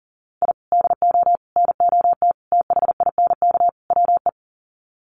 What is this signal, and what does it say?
Morse code, heard as a keyed tone, about 40 wpm, 700 Hz. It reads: I DO NOT THINK WE
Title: Sets of 5 Words 40wpm